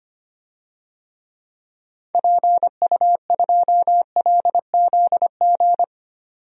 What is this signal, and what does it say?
Morse code, heard as a keyed tone, about 25 wpm, 700 Hz. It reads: PU2LZG
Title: DXCC-100 Call Signs - 6 Characters - 25wpm